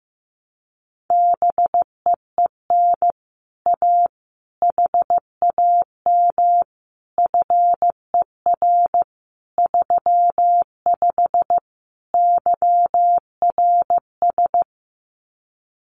Morse code, heard as a keyed tone, about 15 wpm, 700 Hz. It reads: BEEN A HAM FER 35 YRS
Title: Common QSO Phrases Encore 15wpm